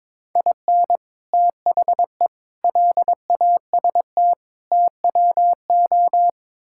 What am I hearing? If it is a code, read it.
Morse code, heard as a keyed tone, about 22 wpm, 700 Hz. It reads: IN THE LAST TWO
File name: Build A Phrase - 5-Word Phrases - 22wpm